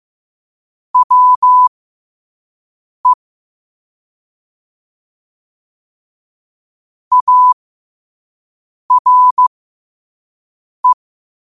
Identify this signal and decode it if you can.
Morse code, heard as a keyed tone, about 15 wpm, spaced as for 5 wpm, 1000 Hz. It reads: WE ARE
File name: qotdfiles / 5wpm